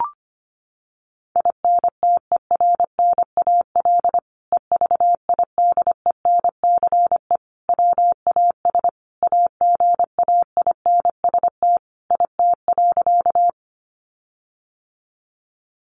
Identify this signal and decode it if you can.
Morse code, heard as a keyed tone, about 25 wpm, 700 Hz. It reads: INTERNAL EVIDENCE WAS AGAINST IT.
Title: Sentences from Top 1400 Words 25wpm